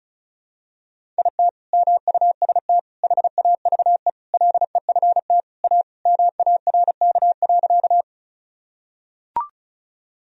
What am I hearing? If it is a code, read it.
Morse code, heard as a keyed tone, about 34 wpm, 700 Hz. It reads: IT MUST HAVE LEFT A MARK.
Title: Sentences from Top 1000 Words - Review Part 1 - 35wpm